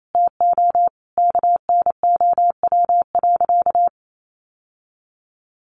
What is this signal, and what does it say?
Morse code, heard as a keyed tone, about 28 wpm, 700 Hz. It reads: TO KNOW.
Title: Sentences from Top 1100 Words 28wpm